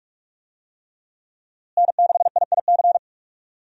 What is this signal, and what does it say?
Morse code, heard as a keyed tone, about 46 wpm, 700 Hz. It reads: N6IIC
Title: USA Call Signs - 5 Characters - Repeated 3x - 45wpm